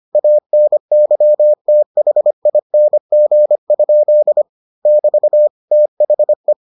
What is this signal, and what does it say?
Morse code, heard as a keyed tone, about 25 wpm, 600 Hz. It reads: ANYTHING? <BT> THE